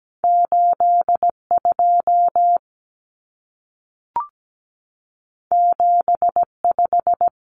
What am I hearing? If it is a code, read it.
Morse code, heard as a keyed tone, about 17 wpm, 700 Hz. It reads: 82 75
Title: Numbers - 2-Digit 17wpm